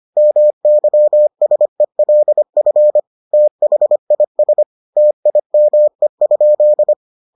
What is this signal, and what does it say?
Morse code, heard as a keyed tone, about 25 wpm, 600 Hz. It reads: MYSELF THIS TIME?